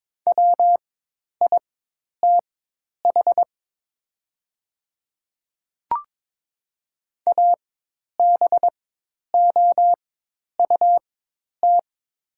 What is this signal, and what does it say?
Morse code, heard as a keyed tone, about 22 wpm, spaced as for 10 wpm, 700 Hz. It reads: WITH ABOUT
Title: 4x Character Spacing - Top 100 Words - 22wpm